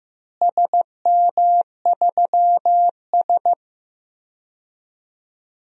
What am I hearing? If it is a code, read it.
Morse code, heard as a keyed tone, about 15 wpm, 700 Hz. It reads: SM3S
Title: DXCC-100 Call Signs - 4 Characters - Repeated 3x - 15wpm